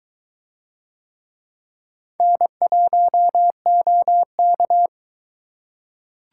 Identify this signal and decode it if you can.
Morse code, heard as a keyed tone, about 22 wpm, 700 Hz. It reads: N1OK